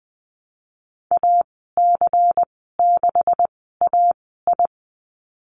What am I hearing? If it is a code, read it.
Morse code, heard as a keyed tone, about 20 wpm, spaced as for 15 wpm, 700 Hz. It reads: AC6AI
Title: 140 Lesson 23g - Letter f - Call Signs 20wpm